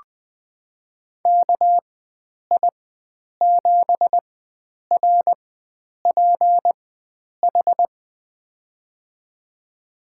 Morse code, heard as a keyed tone, about 20 wpm, spaced as for 9 wpm, 700 Hz. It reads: KI7RPH
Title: USA Call Signs - 6 Characters - 4x Character Spacing - 20wpm